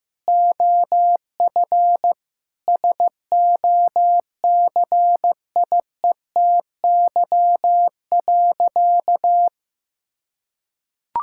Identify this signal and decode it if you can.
Morse code, heard as a keyed tone, about 16 wpm, 700 Hz. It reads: OF SOCIETY.
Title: Sentences from Top 1000 Words - Review Part 5 - 15wpm